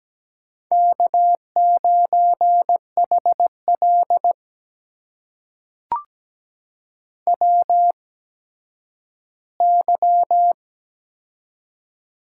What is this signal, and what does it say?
Morse code, heard as a keyed tone, about 17 wpm, spaced as for 10 wpm, 700 Hz. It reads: K9HL WY
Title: USA Call Signs - 4 Characters - 8x Character Spacing - 17wpm